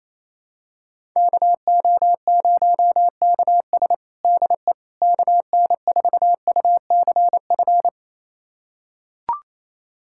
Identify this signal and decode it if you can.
Morse code, heard as a keyed tone, about 28 wpm, 700 Hz. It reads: KO0KS DE KN4UCF
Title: USA Callsign DE Callsign 28wpm